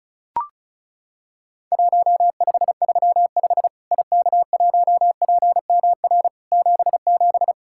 Morse code, heard as a keyed tone, about 36 wpm, 700 Hz. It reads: 1535 IK1PMR 77
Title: Worked All Europe DX Contest - QTC Exchange - 35wpm